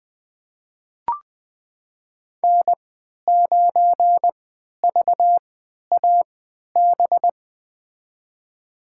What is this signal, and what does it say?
Morse code, heard as a keyed tone, about 20 wpm, spaced as for 11 wpm, 700 Hz. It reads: N9VAB
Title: USA Call Signs - 5 Characters - 3x Character Spacing - 20wpm